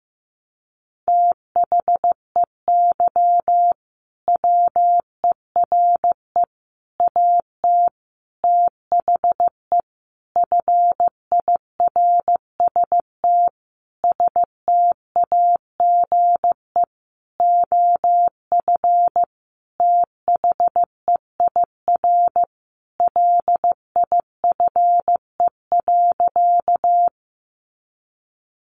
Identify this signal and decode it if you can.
Morse code, heard as a keyed tone, about 15 wpm, 700 Hz. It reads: THEY WERE AT THE FIRST STAGE OF THEIR LIFE.
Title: Sentences from Top 1200 Words 15wpm